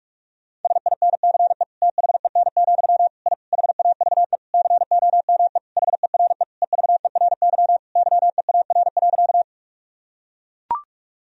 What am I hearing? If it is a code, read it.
Morse code, heard as a keyed tone, about 41 wpm, 700 Hz. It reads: SINCE THEN, I HAVE COME HERE EVERY YEAR.
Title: Sentences from Top 1000 Words - Review Part 1 - 45wpm